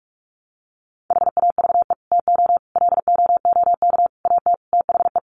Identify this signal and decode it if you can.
Morse code, heard as a keyed tone, about 44 wpm, 700 Hz. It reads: HAVE TO LOOK AT THE
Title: Sets of 5 Words Encore 45wpm